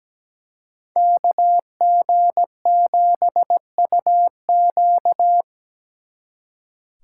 Morse code, heard as a keyed tone, about 20 wpm, 700 Hz. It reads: KG7UQ
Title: Speed Racing - USA 2x2 Call Signs 17wpm